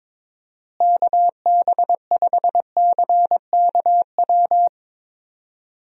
Morse code, heard as a keyed tone, about 22 wpm, 700 Hz. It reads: KB5CKW
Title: USA Call Signs - 6 Characters - Repeated 2x - 22wpm